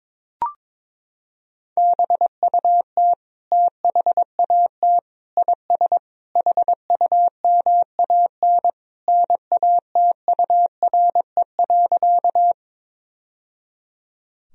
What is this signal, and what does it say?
Morse code, heard as a keyed tone, about 22 wpm, 700 Hz. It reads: BUT THAT IS HUMAN NATURE.
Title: Sentences from Top 1000 Words - Review Part 3 - 22wpm